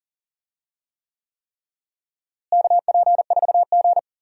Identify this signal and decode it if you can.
Morse code, heard as a keyed tone, about 40 wpm, 700 Hz. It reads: KP4G
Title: DXCC-100 Call Signs - 4 Characters - Repeated 3x - 40wpm